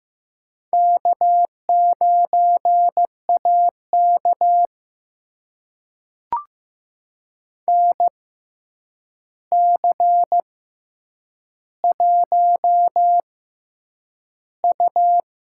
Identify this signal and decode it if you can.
Morse code, heard as a keyed tone, about 15 wpm, spaced as for 8 wpm, 700 Hz. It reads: K9AK NC1U
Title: USA Call Signs - 4 Characters - 6x Character Spacing - 15wpm